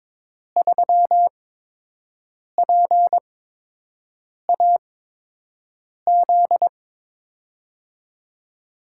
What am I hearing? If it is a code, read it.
Morse code, heard as a keyed tone, about 22 wpm, spaced as for 6 wpm, 700 Hz. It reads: 3PAZ
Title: USA Call Signs - 5 Characters - 8x Character Spacing - 22wpm